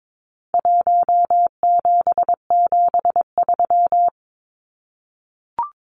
Morse code, heard as a keyed tone, about 22 wpm, 700 Hz. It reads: 1773
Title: Numbers - Contest Style - 4-Digit 22wpm